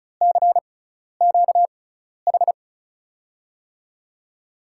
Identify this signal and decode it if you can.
Morse code, heard as a keyed tone, about 36 wpm, spaced as for 12 wpm, 700 Hz. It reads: CQH
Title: USA Call Signs - 6x Character Spacing - 35wpm